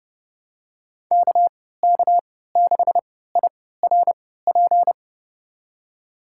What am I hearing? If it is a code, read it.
Morse code, heard as a keyed tone, about 29 wpm, spaced as for 17 wpm, 700 Hz. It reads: KK6IRP
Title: USA Call Signs - 6 Characters - 3x Character Spacing - 30wpm